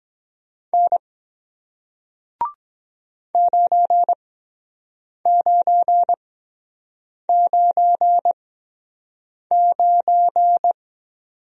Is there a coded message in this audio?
Yes: Morse code from 0.7 to 10.7 s, heard as a keyed tone, about 22 wpm, spaced as for 7 wpm, 700 Hz. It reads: N 9999